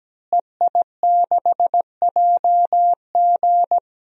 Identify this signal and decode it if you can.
Morse code, heard as a keyed tone, about 17 wpm, 700 Hz. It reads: EI6JG
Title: DXCC-100 Call Signs - 5 Characters - Repeated 3x - 17wpm